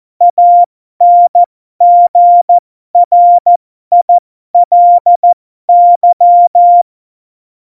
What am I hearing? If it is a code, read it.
Morse code, heard as a keyed tone, about 14 wpm, spaced as for 12 wpm, 700 Hz. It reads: ANGRILY
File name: Morsekurs / Witze